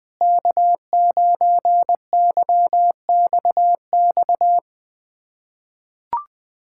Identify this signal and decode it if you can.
Morse code, heard as a keyed tone, about 20 wpm, 700 Hz. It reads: K9YXX
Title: USA Callsign DE Callsign - 5 Characters - 20wpm